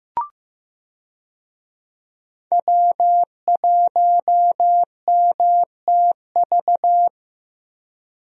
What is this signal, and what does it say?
Morse code, heard as a keyed tone, about 16 wpm, 700 Hz. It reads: W1MTV
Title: USA Call Signs - 5 Characters - Repeated 3x - 15wpm